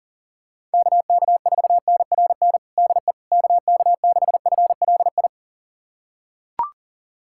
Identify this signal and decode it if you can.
Morse code, heard as a keyed tone, about 40 wpm, 700 Hz. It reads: KK4NRN DE KK6FLI
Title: USA Callsign DE Callsign - 6 Characters - 40wpm